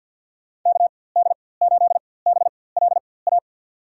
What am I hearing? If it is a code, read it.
Morse code, heard as a keyed tone, about 53 wpm, spaced as for 24 wpm, 700 Hz. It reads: KD8BLA
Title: USA Call Signs - 6 Characters - 4x Character Spacing - 50wpm